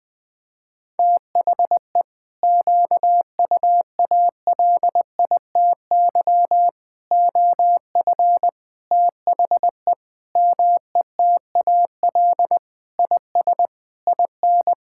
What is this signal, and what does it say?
Morse code, heard as a keyed tone, about 20 wpm, 700 Hz. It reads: THE QUALITY OF THE METAL IS IN
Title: Sentences from Top 1000 Words - Review Part 6 - 20wpm